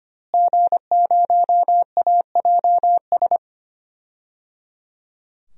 Morse code, heard as a keyed tone, about 25 wpm, 700 Hz. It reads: G0AJS
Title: DXCC-100 Call Signs - 5 Characters - Repeated 3x - 25wpm